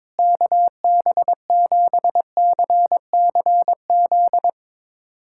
Morse code, heard as a keyed tone, about 21 wpm, 700 Hz. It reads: KB7CCZ
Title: USA Callsign DE Callsign - 6 Characters - 22wpm